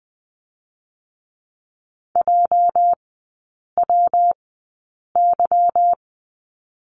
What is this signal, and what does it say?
Morse code, heard as a keyed tone, about 20 wpm, spaced as for 8 wpm, 690 Hz. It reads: JWY
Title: 205 Lesson 33e - Letter j - Three Characters 20wpm